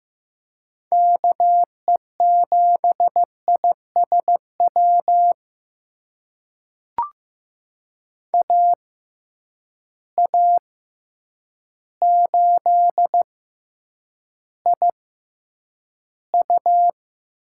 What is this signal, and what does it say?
Morse code, heard as a keyed tone, about 15 wpm, spaced as for 8 wpm, 700 Hz. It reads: KE7ISW AA8IU
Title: USA Call Signs - 6x Character Spacing - 15wpm